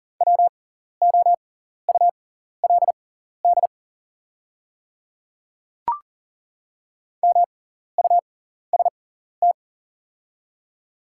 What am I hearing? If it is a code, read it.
Morse code, heard as a keyed tone, about 41 wpm, spaced as for 14 wpm, 690 Hz. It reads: WOULD MUST
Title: 6x Character Spacing - Top 300 Words - 40wpm